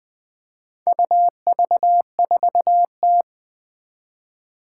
Morse code, heard as a keyed tone, about 20 wpm, 700 Hz. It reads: UV4T
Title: Four Challenging Letter-Number Combinations 20wpm